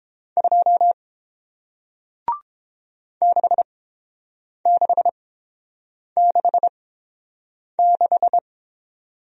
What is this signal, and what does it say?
Morse code, heard as a keyed tone, about 30 wpm, spaced as for 7 wpm, 700 Hz. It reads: 2 6666